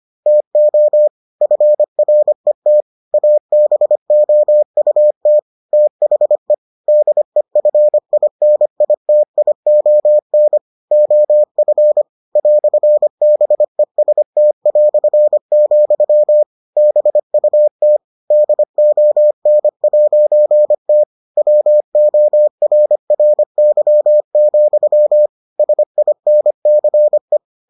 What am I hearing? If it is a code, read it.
Morse code, heard as a keyed tone, about 25 wpm, 600 Hz. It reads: TO FRET ABOUT THE DEFINITION OF "BEST", BUT DON'T WORRY, SINCE